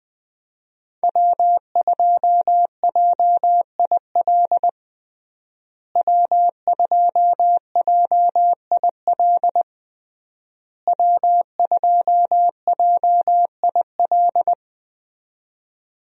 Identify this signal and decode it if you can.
Morse code, heard as a keyed tone, about 20 wpm, 700 Hz. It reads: W2JIL W2JIL W2JIL
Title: USA Call Signs - 5 Characters - Repeated 3x - 20wpm